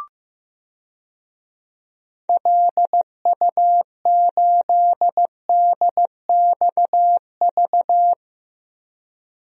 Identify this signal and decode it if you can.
Morse code, heard as a keyed tone, about 15 wpm, 700 Hz. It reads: LU8DXV
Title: DXCC-100 Call Signs - 6 Characters - Repeated 2x - 15wpm